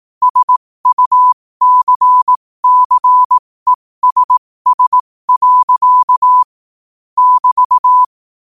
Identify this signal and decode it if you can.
Morse code, heard as a keyed tone, about 18 wpm, spaced as for 15 wpm, 1000 Hz. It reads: SUCCESS. <BT>
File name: qotdfiles / 15wpm